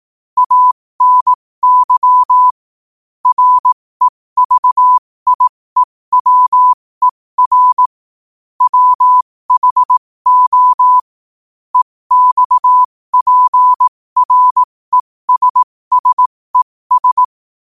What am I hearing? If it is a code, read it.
Morse code, heard as a keyed tone, about 18 wpm, spaced as for 15 wpm, 1000 Hz. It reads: ANY REVIEWER WHO EXPRESSES
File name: qotdfiles / 15wpm